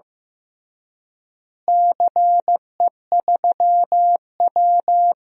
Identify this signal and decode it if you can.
Morse code, heard as a keyed tone, about 15 wpm, 700 Hz. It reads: CE3W
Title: DXCC-100 Call Signs - 4 Characters - Repeated 3x - 15wpm